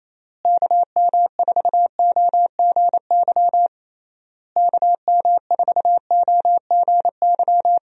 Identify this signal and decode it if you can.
Morse code, heard as a keyed tone, about 27 wpm, 700 Hz. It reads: KM4OGY KM4OGY
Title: USA Call Signs - 6 Characters - Repeated 3x - 28wpm